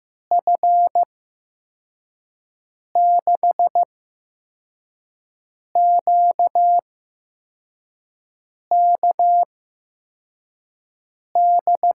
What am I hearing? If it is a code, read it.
Morse code, heard as a keyed tone, about 15 wpm, spaced as for 4 wpm, 700 Hz. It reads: F6QKD
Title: USA Call Signs - 6 Characters - 8x Character Spacing - 15wpm